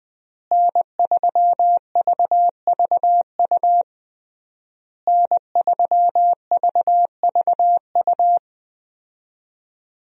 Morse code, heard as a keyed tone, about 20 wpm, 700 Hz. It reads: N3VVU N3VVU
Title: USA Call Signs - 5 Characters - Repeated 3x - 20wpm